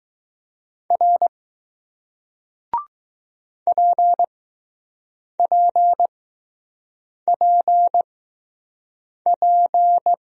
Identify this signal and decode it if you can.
Morse code, heard as a keyed tone, about 20 wpm, spaced as for 6 wpm, 700 Hz. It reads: R PPPP